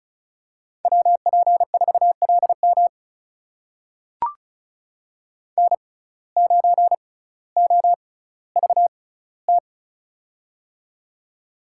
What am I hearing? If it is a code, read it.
Morse code, heard as a keyed tone, about 34 wpm, spaced as for 18 wpm, 700 Hz. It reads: WP4LM N9OVT
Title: USA Call Signs - 5 Characters - 6x Character Spacing - 35wpm